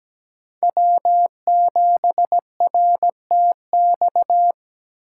Morse code, heard as a keyed tone, about 17 wpm, 700 Hz. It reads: W7RTX
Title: USA Call Signs - 5 Characters - 6x Character Spacing - 17wpm